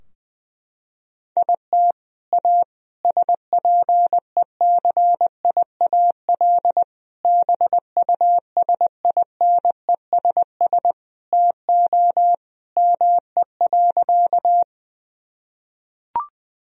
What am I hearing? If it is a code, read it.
Morse code, heard as a keyed tone, about 21 wpm, 700 Hz. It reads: IT A SPECIAL BUSINESS TO ME.
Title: Sentences from Top 1000 Words - Review Part 3 - 20wpm